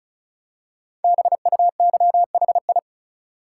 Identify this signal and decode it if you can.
Morse code, heard as a keyed tone, about 36 wpm, 700 Hz. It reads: BUYHI
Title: Code Groups - 5 Letters - 35wpm